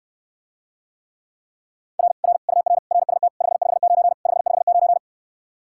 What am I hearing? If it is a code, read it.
Morse code, heard as a keyed tone, about 69 wpm, 700 Hz. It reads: R R UR RST 559 559